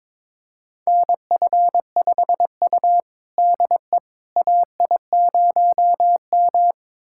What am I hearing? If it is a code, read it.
Morse code, heard as a keyed tone, about 22 wpm, 700 Hz. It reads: NF5U DE AI0M
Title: USA Callsign DE Callsign - 4 Characters - 22wpm